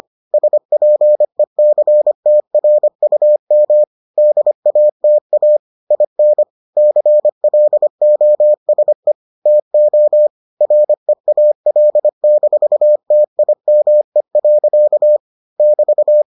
Morse code, heard as a keyed tone, about 25 wpm, 600 Hz. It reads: SPECTRUM DATA IN CLOSE TO REAL-TIME. <BT>